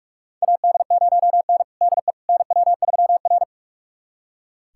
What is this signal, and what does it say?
Morse code, heard as a keyed tone, about 42 wpm, 700 Hz. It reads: AD0N DE NW3R
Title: USA Callsign DE Callsign - 4 Characters - 45wpm